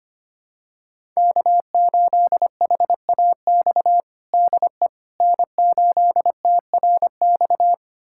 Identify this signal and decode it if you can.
Morse code, heard as a keyed tone, about 25 wpm, 700 Hz. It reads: K8HAX DE N8TRX
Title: USA Callsign DE Callsign - 5 Characters - 25wpm